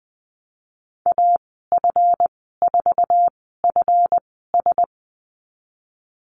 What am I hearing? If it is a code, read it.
Morse code, heard as a keyed tone, about 20 wpm, spaced as for 15 wpm, 700 Hz. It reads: AF4FS
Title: 154 Lesson 25g - Letter p - Call Signs 20wpm